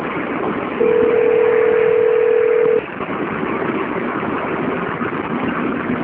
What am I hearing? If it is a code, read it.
Ringback tone (North America), 440 + 480 Hz, with noise.
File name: out- - - - - .29641